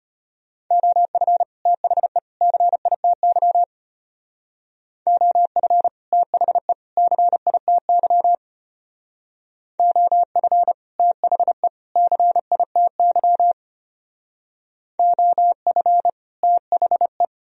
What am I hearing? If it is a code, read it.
Morse code, heard as a keyed tone, about 32 wpm, 700 Hz. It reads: OF THE CITY OF THE CITY OF THE CITY OF THE